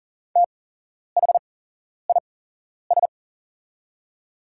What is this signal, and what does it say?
Morse code, heard as a keyed tone, about 41 wpm, spaced as for 11 wpm, 690 Hz. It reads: THIS
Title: 8x Character Spacing - Top 500 Words - 40wpm